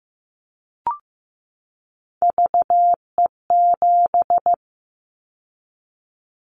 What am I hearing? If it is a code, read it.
Morse code, heard as a keyed tone, about 15 wpm, 700 Hz. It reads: VE7
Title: Callsign Prefixes for Top 50 DXCC - 15wpm